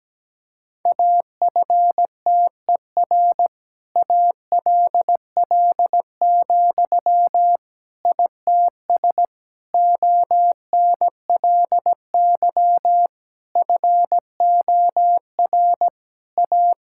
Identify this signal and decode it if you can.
Morse code, heard as a keyed tone, about 17 wpm, 700 Hz. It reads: AFTER ALL, ITS ONLY FOR A
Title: Sentences from Top 1000 Words - Review Part 5 - 17wpm